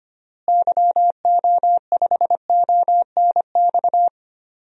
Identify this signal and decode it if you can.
Morse code, heard as a keyed tone, about 25 wpm, 700 Hz. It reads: YO5ONX
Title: DXCC-100 Call Signs - 6 Characters - Repeated 2x - 25wpm